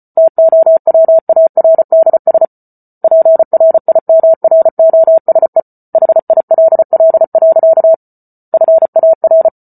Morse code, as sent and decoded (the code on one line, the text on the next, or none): - --- .-- .- .-. -.. ... / .--. .-. .. -- .-. --- ... / .... .. .-.. .-.. .-.-.- / ..-. .- .-.
TOWARDS PRIMROSE HILL. FAR